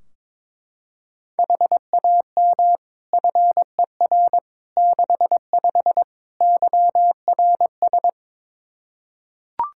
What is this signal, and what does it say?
Morse code, heard as a keyed tone, about 22 wpm, 700 Hz. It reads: HAM FER 65 YRS
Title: Practice QSO - Exchange 3 - More Info - 22wpm